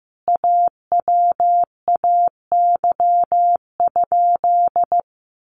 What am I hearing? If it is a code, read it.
Morse code, heard as a keyed tone, about 15 wpm, 700 Hz. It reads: AWAY?
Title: Sentences from Top 1200 Words 15wpm